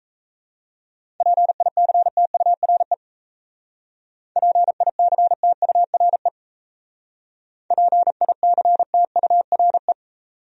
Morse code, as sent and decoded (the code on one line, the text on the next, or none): .--. .. -.-. - ..- .-. / .--. .. -.-. - ..- .-. / .--. .. -.-. - ..- .-. .
PICTURE PICTURE PICTURE